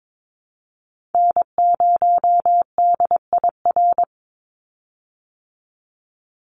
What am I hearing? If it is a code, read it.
Morse code, heard as a keyed tone, about 22 wpm, 700 Hz. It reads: N0DIR
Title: Call Signs Repeated 2x - 22wpm